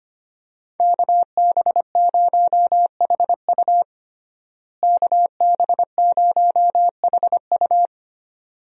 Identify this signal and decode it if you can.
Morse code, heard as a keyed tone, about 25 wpm, 700 Hz. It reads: KB0HU KB0HU
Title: USA Call Signs - 5 Characters - Repeated 2x - 25wpm